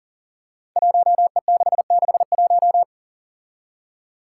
Morse code, heard as a keyed tone, about 38 wpm, 700 Hz. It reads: 1E661
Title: Five Challenging Letter-Number Combinations 40wpm